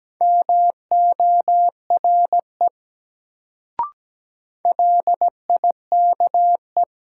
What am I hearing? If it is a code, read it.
Morse code, heard as a keyed tone, about 17 wpm, 700 Hz. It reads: MORE LIKE
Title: Sets of 2 Words - New Word Drill - 17wpm